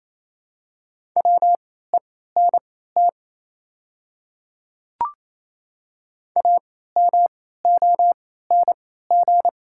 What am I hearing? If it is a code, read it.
Morse code, heard as a keyed tone, about 28 wpm, spaced as for 16 wpm, 700 Hz. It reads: WENT AMONG
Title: 3x Character Spacing - Top 500 Words - 28wpm